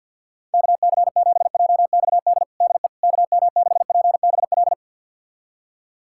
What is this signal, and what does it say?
Morse code, heard as a keyed tone, about 51 wpm, 700 Hz. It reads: KC7JXD DE KM6PBL
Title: USA Callsign DE Callsign - 6 Characters - 50wpm